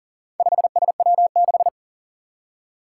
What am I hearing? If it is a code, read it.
Morse code, heard as a keyed tone, about 40 wpm, 700 Hz. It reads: 5SW6
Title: Four Challenging Letter-Number Combinations 40wpm